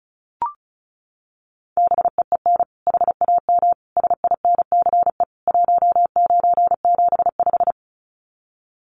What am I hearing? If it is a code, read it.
Morse code, heard as a keyed tone, about 34 wpm, 700 Hz. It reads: BEEN HAM SINCE 1975
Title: Common QSO Phrases 35wpm